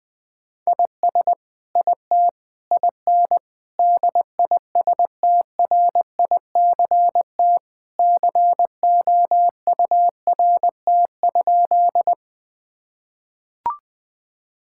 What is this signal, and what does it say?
Morse code, heard as a keyed tone, about 20 wpm, 700 Hz. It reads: IS IT IN DISTRICT COURT?
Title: Sentences from Top 1600 Words 20wpm